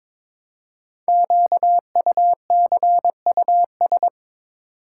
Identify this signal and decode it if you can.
Morse code, heard as a keyed tone, about 23 wpm, 700 Hz. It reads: QUCUS
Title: Code Groups - 5 Letters - 22wpm